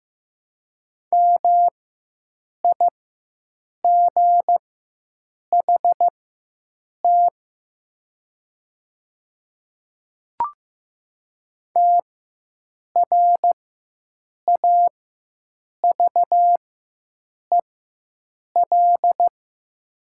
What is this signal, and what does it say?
Morse code, heard as a keyed tone, about 15 wpm, spaced as for 7 wpm, 700 Hz. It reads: MIGHT TRAVEL
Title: 4x Character Spacing - Top 500 Words - 15wpm